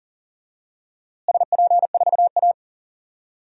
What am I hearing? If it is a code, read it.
Morse code, heard as a keyed tone, about 40 wpm, 700 Hz. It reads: SP4A
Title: DXCC-100 Call Signs - 4 Characters - Repeated 3x - 40wpm